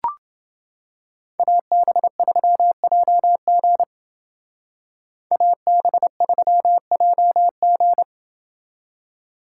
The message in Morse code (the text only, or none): AB3JG AB3JG